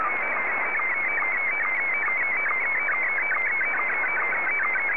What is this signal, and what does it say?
SSTV (slow-scan television), Scottie 1.